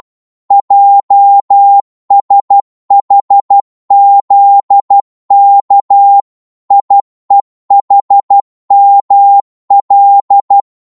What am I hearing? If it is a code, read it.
Morse code, heard as a keyed tone, about 12 wpm, 800 Hz. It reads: JSHZK IEHML
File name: Morse / ltr-12wpm